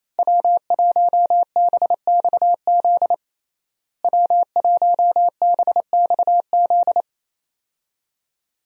Morse code, heard as a keyed tone, about 28 wpm, 700 Hz. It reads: W1BXZ W1BXZ
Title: USA Call Signs - 5 Characters - Repeated 2x - 28wpm